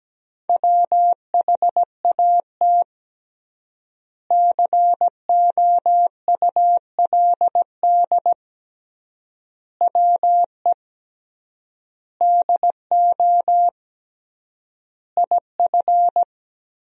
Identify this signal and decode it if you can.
Morse code, heard as a keyed tone, about 17 wpm, 700 Hz. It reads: WHAT COULD WE DO IF
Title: 3x Word-Spacing - Sentences from Top 400 Words - 17wpm